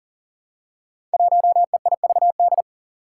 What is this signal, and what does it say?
Morse code, heard as a keyed tone, about 40 wpm, 700 Hz. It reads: 1EIVD
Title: Five Challenging Letter-Number Combinations 40wpm